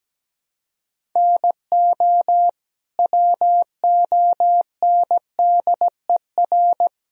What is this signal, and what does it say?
Morse code, heard as a keyed tone, about 17 wpm, 700 Hz. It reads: NO WONDER
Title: Sentences from Top 1000 Words - Review Part 3 - 17wpm